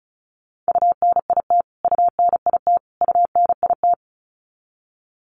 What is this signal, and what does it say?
Morse code, heard as a keyed tone, about 34 wpm, 700 Hz. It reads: UNIT UNIT UNIT
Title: Top 500 Words Repeated 35wpm